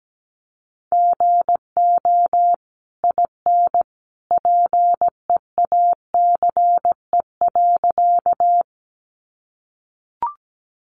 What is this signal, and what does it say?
Morse code, heard as a keyed tone, about 17 wpm, 700 Hz. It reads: GO IN PEACE.
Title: Sentences from Top 1500 Words 17wpm